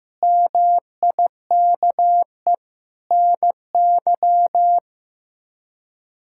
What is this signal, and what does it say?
Morse code, heard as a keyed tone, about 15 wpm, 700 Hz. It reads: MIKE NY
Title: K1USN SST QSO Practice - 15wpm